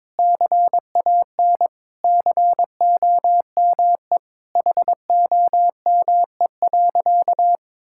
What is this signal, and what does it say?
Morse code, heard as a keyed tone, about 22 wpm, 700 Hz. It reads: CAN COME HOME.
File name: practice / 22wpm / Sentences with 3-Word Phrases - 22wpm